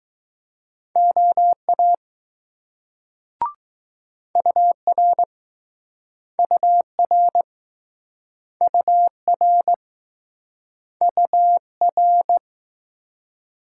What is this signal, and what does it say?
Morse code, heard as a keyed tone, about 20 wpm, 700 Hz. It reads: OA UR UR UR UR